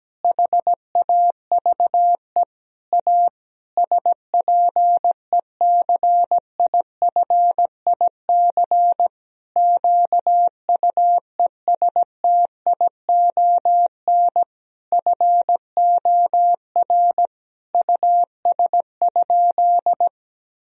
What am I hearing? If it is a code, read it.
Morse code, heard as a keyed tone, about 17 wpm, 700 Hz. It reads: HAVE A SPECIFIC QUESTION FOR US?
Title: Sentences from Top 1000 Words - Review Part 3 - 17wpm